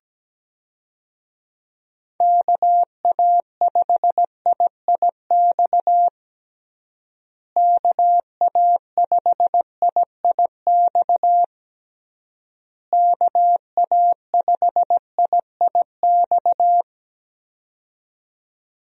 Morse code, heard as a keyed tone, about 17 wpm, 700 Hz. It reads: KA5IIX KA5IIX KA5IIX
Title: USA Call Signs - 6 Characters - Repeated 3x - 17wpm